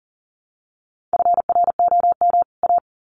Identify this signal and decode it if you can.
Morse code, heard as a keyed tone, about 40 wpm, 700 Hz. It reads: FROM A
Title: Sets of 2 Words 40wpm